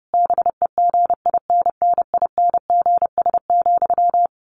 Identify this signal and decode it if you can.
Morse code, heard as a keyed tone, about 30 wpm, 700 Hz. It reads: BEGINNINGS,
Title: A Sound Of Thunder 30wpm